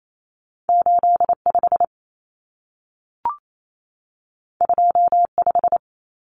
Morse code, heard as a keyed tone, about 28 wpm, 700 Hz. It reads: 85 25
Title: Common Cut Numbers - 2-Digit 28wpm